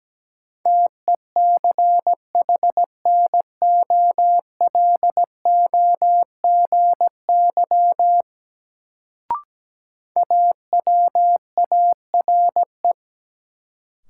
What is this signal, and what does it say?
Morse code, heard as a keyed tone, about 17 wpm, 700 Hz. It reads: TECHNOLOGY AWARE
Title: Sentences from Top 1200 Words - New Word Drill Alt - 17wpm